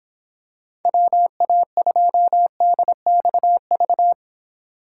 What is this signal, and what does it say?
Morse code, heard as a keyed tone, about 26 wpm, 700 Hz. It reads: WA2DXV